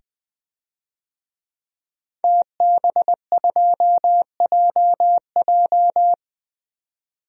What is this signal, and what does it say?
Morse code, heard as a keyed tone, about 20 wpm, 700 Hz. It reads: TB2JJ
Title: DXCC-100 Call Signs - 5 Characters - 20wpm